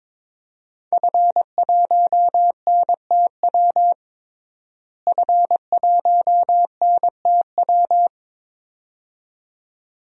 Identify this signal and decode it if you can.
Morse code, heard as a keyed tone, about 22 wpm, 700 Hz. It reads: F1NTW F1NTW
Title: DXCC-100 Call Signs - 5 Characters - Repeated 2x - 22wpm